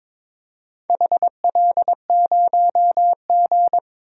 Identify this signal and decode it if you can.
Morse code, heard as a keyed tone, about 22 wpm, 700 Hz. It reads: HL0G
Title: DXCC-100 Call Signs - 4 Characters - Repeated 3x - 22wpm